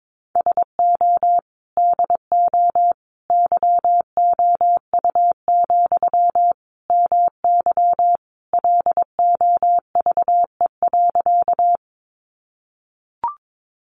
Morse code, heard as a keyed tone, about 22 wpm, 700 Hz. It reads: SO DO YOU, MY LOVE.
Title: Sentences from Top 400 Words 22wpm